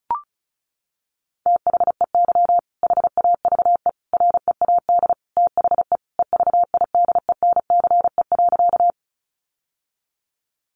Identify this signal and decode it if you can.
Morse code, heard as a keyed tone, about 34 wpm, 700 Hz. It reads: THEY HAVE READ THE EVIDENCE.
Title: Sentences from Top 1300 Words 35wpm